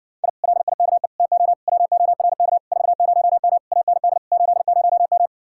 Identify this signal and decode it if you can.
Morse code, heard as a keyed tone, about 60 wpm, 700 Hz. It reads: I LIKE TO WORK 40M AND 80M